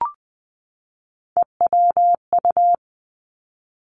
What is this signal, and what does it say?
Morse code, heard as a keyed tone, about 20 wpm, 700 Hz. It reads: EWU
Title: Three Challenging Letter-Number Combinations 20wpm